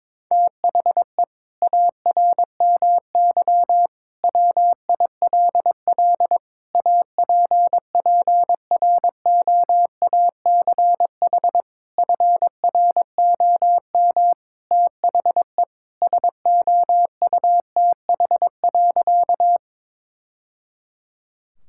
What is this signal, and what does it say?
Morse code, heard as a keyed tone, about 21 wpm, 700 Hz. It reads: THE ARMY WILL APPROACH FROM THE SOUTH.
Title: Sentences from Top 1000 Words - Review Part 6 - 22wpm